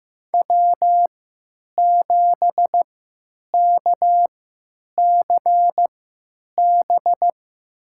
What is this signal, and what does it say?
Morse code, heard as a keyed tone, about 15 wpm, spaced as for 9 wpm, 700 Hz. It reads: W7KCB
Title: USA Call Signs - 3x Character Spacing - 15wpm